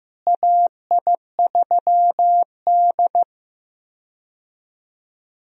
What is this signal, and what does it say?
Morse code, heard as a keyed tone, about 16 wpm, 700 Hz. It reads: AI3D
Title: USA Call Signs - 4 Characters - Repeated 3x - 15wpm